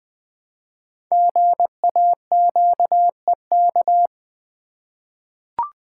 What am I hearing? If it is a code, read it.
Morse code, heard as a keyed tone, about 20 wpm, 700 Hz. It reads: GAQEK
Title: Code Groups - 5 Letters - 20wpm